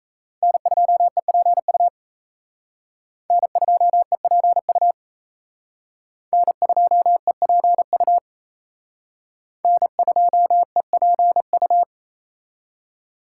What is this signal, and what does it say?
Morse code, heard as a keyed tone, about 35 wpm, 700 Hz. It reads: N2EPU N2EPU N2EPU N2EPU